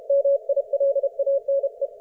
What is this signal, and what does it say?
Morse code, heard as a keyed tone, about 28 wpm, 550 Hz. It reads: MILANE
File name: Archiv / CUC216 / ok1fhi cul216